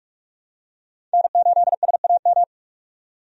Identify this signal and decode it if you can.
Morse code, heard as a keyed tone, about 46 wpm, 700 Hz. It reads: N8SAM
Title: USA Call Signs - 5 Characters - 45wpm